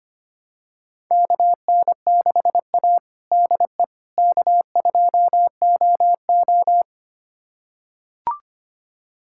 Morse code, heard as a keyed tone, about 25 wpm, 700 Hz. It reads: KN6A DE K2OO
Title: USA Callsign DE Callsign - 4 Characters - 25wpm